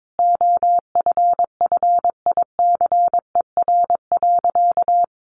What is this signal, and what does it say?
Morse code, heard as a keyed tone, about 22 wpm, 700 Hz. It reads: OFFICER.
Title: Sentences from Top 1200 Words 22wpm